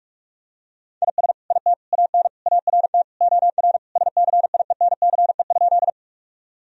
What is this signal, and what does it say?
Morse code, heard as a keyed tone, about 41 wpm, 700 Hz. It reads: IS IT AN ART OR SCIENCE?
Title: Sentences from Top 1000 Words - Review Part 3 - 45wpm